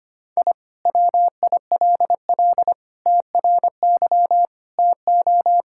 Morse code, heard as a keyed tone, about 25 wpm, 700 Hz. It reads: I WILL TRY TO
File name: practice / 25wpm / Sentences with 3-Word Phrases - 25wpm